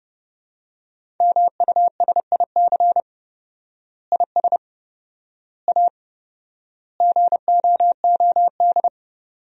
Morse code, heard as a keyed tone, about 32 wpm, 700 Hz. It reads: MUSIC IS A GOOD
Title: 4x Word-Spacing - Sentences from Top 300 Words - 30wpm